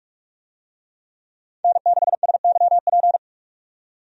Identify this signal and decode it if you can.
Morse code, heard as a keyed tone, about 44 wpm, 700 Hz. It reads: N6SYP
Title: USA Call Signs - 5 Characters - 45wpm